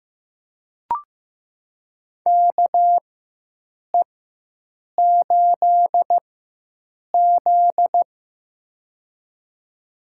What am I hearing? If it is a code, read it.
Morse code, heard as a keyed tone, about 15 wpm, spaced as for 7 wpm, 700 Hz. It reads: KE8Z
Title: USA Call Signs - 4 Characters - 4x Character Spacing - 15wpm